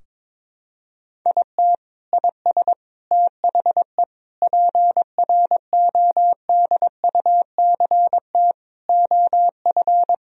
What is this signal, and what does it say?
Morse code, heard as a keyed tone, about 21 wpm, 700 Hz. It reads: IT IS THE PRODUCT OF
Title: Sentences from Top 1000 Words - Review Part 2 - 22wpm